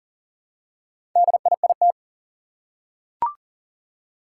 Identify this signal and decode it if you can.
Morse code, heard as a keyed tone, about 42 wpm, 700 Hz. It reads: DIIT
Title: Four Challenging Letter-Number Combinations 40wpm